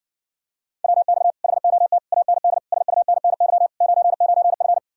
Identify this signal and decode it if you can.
Morse code, heard as a keyed tone, about 49 wpm, 700 Hz. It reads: WX HOT AND SUNNY 99F
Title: Speed-Racing Common QSO Phrases - 40wpm